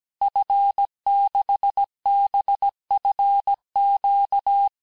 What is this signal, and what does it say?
Morse code, heard as a keyed tone, about 17 wpm, 800 Hz. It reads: F6BFQ
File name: corrige-cours-1581-fin-du-QSO-vitesse-16-mpm